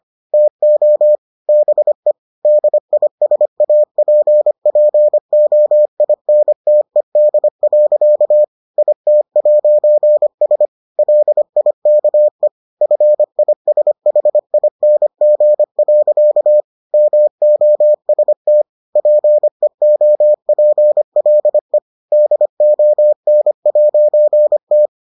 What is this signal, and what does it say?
Morse code, heard as a keyed tone, about 25 wpm, 600 Hz. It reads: TO BE DISAPPOINTED. IT'S LIKE FISHING. MOST PEOPLE DON'T